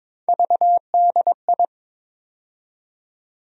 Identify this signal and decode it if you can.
Morse code, heard as a keyed tone, about 22 wpm, 700 Hz. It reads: VDI
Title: USA Call Sign Postfixes - 3 Letters 22wpm